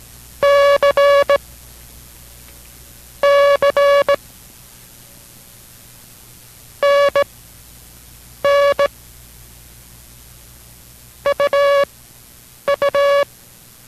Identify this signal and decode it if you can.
Morse code, heard as a keyed tone, about 18 wpm, spaced as for 6 wpm, 570 Hz. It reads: CC NN UU